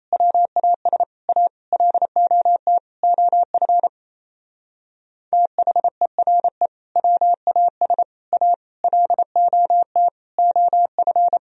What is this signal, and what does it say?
Morse code, heard as a keyed tone, about 34 wpm, 700 Hz. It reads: WAS A LOT OF THERE WAS A LOT OF
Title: Speed-Racing - Sets of 5 Words 28wpm